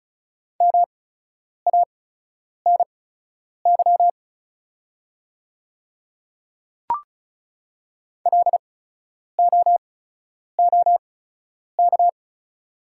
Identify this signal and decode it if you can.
Morse code, heard as a keyed tone, about 34 wpm, spaced as for 9 wpm, 690 Hz. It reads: MANY LOOK
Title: 8x Character Spacing - Top 200 Words - 35wpm